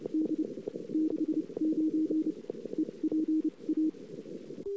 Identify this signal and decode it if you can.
Morse code, heard as a keyed tone, about 29 wpm, spaced as for 23 wpm, 310 Hz. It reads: D 69 EPA
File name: CW / 2008-ss-cw-k5tr-op-n5rz -21:00